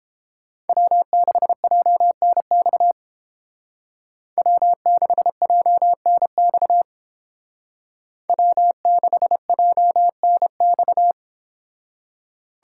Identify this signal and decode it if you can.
Morse code, heard as a keyed tone, about 30 wpm, 700 Hz. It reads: W6JNX W6JNX W6JNX